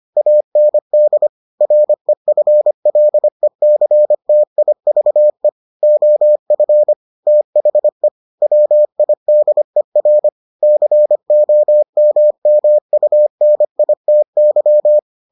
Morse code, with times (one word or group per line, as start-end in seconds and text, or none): 0.2-1.3 AND
1.6-5.5 REFLECTIVE
5.8-6.9 OF
7.3-8.1 THE
8.4-10.3 WIDER
10.6-15.0 COMMUNITY